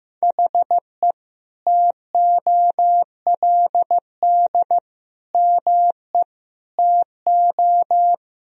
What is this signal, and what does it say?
Morse code, heard as a keyed tone, about 16 wpm, 700 Hz. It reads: HE TOLD ME TO
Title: Sentences from Top 1000 Words - Review Part 3 - 15wpm